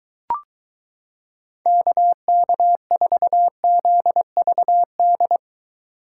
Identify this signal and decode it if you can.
Morse code, heard as a keyed tone, about 23 wpm, 700 Hz. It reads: KK4ZVD